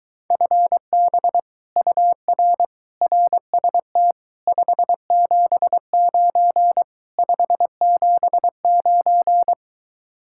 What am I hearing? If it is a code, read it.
Morse code, heard as a keyed tone, about 23 wpm, 700 Hz. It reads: FB UR RST 579 579